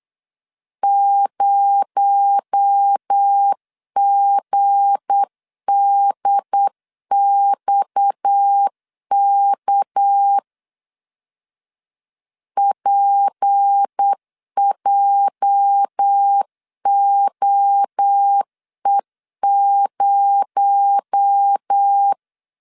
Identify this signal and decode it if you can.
Morse code, heard as a keyed tone, about 8 wpm, 800 Hz. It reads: 0GDXK PJOE0
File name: LEZIONI / 8 YJ45 / 8 YJ 8wpm 3313